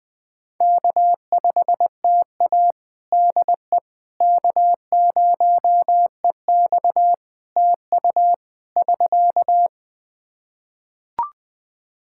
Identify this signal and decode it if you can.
Morse code, heard as a keyed tone, about 20 wpm, 700 Hz. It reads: K5TA DE K0EX TU <SK>
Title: Practice QSO - Exchange 4 - Ending - 20wpm